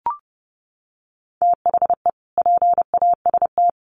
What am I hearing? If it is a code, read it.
Morse code, heard as a keyed tone, about 30 wpm, 700 Hz. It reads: THE PAST
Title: Sets of 2 Words 30wpm